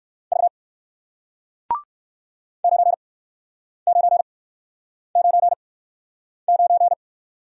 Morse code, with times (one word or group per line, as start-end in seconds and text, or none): 0.3-0.5 V
2.6-6.9 9999